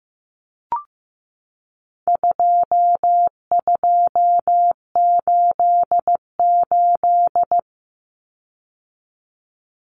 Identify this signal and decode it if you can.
Morse code, heard as a keyed tone, about 15 wpm, 700 Hz. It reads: 2288
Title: Common Cut Numbers - 4-Digit 15wpm